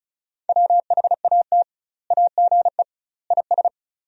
Morse code, heard as a keyed tone, about 36 wpm, 700 Hz. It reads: WHAT AGE IS
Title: 2x Word-Spacing - Sentences from Top 500 Words - 35wpm